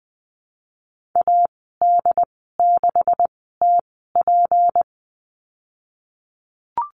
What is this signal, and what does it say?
Morse code, heard as a keyed tone, about 20 wpm, spaced as for 14 wpm, 700 Hz. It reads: AD6TP
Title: 167 Lesson 27g - Number 7 - Call Signs 20wpm